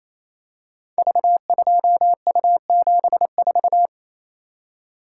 Morse code, heard as a keyed tone, about 28 wpm, 700 Hz. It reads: V2U74
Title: Code Groups - 5 Letter:Numbers - 28wpm